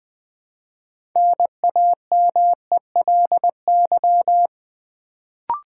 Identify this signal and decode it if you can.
Morse code, heard as a keyed tone, about 20 wpm, 700 Hz. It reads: NAMELY
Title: Build A Word - 3 Letters At A Time - 6-Letter Words - 20wpm